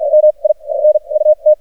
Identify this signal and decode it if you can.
Morse code, heard as a keyed tone, about 47 wpm, spaced as for 30 wpm, 610 Hz, with radio noise. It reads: MINUT
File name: archived / CRSnet